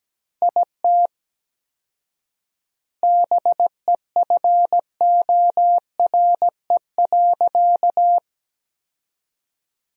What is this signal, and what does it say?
Morse code, heard as a keyed tone, about 17 wpm, 700 Hz. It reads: IT BEFORE.
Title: 4x Word-Spacing - Sentences from Top 400 Words - 17wpm